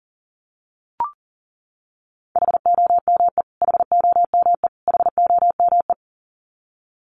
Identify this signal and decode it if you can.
Morse code, heard as a keyed tone, about 40 wpm, 700 Hz. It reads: HOME HOME HOME
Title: Top 300 Words Repeated 40wpm